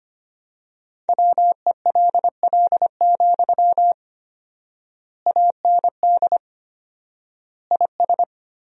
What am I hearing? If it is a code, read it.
Morse code, heard as a keyed tone, about 25 wpm, 700 Hz. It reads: WELL, AND IS
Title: 4x Word-Spacing - Sentences from Top 200 Words - 25wpm